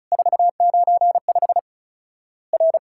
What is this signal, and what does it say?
Morse code, heard as a keyed tone, about 36 wpm, 700 Hz. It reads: 495 R
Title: Worked All Europe DX Contest - QTC Exchange - 35wpm